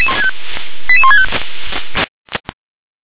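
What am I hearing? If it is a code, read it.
Five-tone selective call (ZVEI), heard as quick runs of five tones, tones of about 70 ms. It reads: 88165
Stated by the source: ZVEI five-tone selective call received on an SDR.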